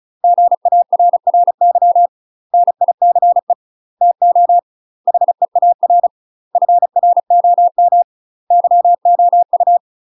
Morse code, heard as a keyed tone, about 34 wpm, 700 Hz. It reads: GARRY NICE TO HEAR FROM YOU
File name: UK MASTER QSOs TRAINING / 35WPM / UK MASTER QSOs 0002